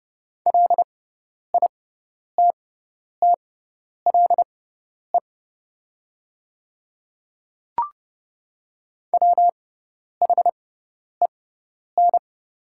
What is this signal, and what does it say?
Morse code, heard as a keyed tone, about 30 wpm, spaced as for 10 wpm, 700 Hz. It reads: LITTLE WHEN
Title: 6x Character Spacing - Top 200 Words - 30wpm